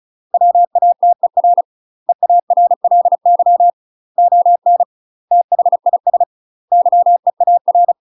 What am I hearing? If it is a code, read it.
Morse code, heard as a keyed tone, about 34 wpm, 700 Hz. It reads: WATER EARLY ON THIS YEAR
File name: UK MASTER QSOs TRAINING / 35WPM / UK MASTER QSOs 0000